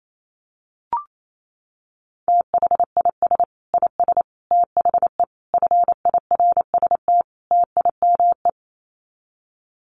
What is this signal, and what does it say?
Morse code, heard as a keyed tone, about 28 wpm, 700 Hz. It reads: THIS IS THE FIRST TIME
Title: Sets of 5 Words 28wpm